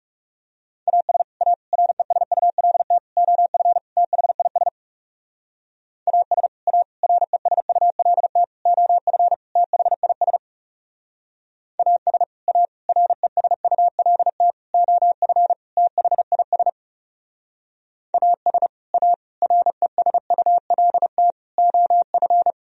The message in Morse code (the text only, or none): AS A RESULT OF THIS AS A RESULT OF THIS AS A RESULT OF THIS AS A RESULT OF